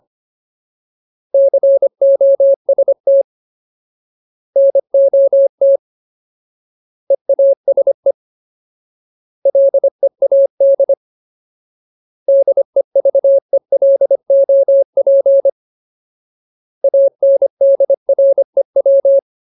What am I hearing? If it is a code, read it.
Morse code, heard as a keyed tone, about 25 wpm, 550 Hz. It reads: COST NOT EASE LEAD DEVELOP ANDREW
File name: Ultimate Word List / 25WPM Ultimate